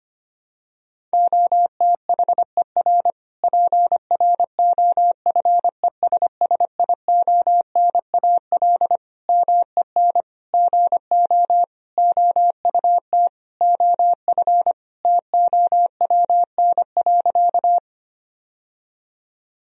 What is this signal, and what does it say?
Morse code, heard as a keyed tone, about 25 wpm, 700 Hz. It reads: OTHER PROFESSIONAL MEN GO OUT OF TOWN.
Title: Sentences from Top 1000 Words - Review Part 4 - 25wpm